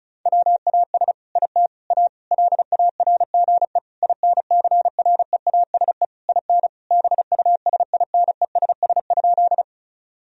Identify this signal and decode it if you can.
Morse code, heard as a keyed tone, about 34 wpm, 700 Hz. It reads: WAS IT A LARGE INCREASE IN BUSINESS?
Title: Sentences from Top 1000 Words - Review Part 4 - 35wpm